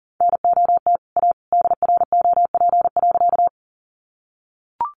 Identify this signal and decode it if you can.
Morse code, heard as a keyed tone, about 40 wpm, 700 Hz. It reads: NOT A DROP.
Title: Sentences from Top 1400 Words 40wpm